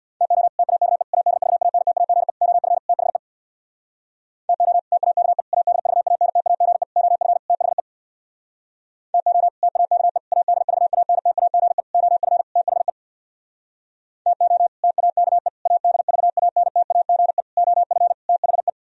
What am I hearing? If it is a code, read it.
Morse code, heard as a keyed tone, about 64 wpm, 700 Hz. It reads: TO TAKE ADVANTAGE OF THE TO TAKE ADVANTAGE OF THE TO TAKE ADVANTAGE OF THE TO TAKE ADVANTAGE OF THE